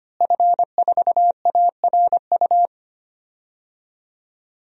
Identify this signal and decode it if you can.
Morse code, heard as a keyed tone, about 25 wpm, 700 Hz. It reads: F4ARU
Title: DXCC-100 Call Signs - 5 Characters - Repeated 3x - 25wpm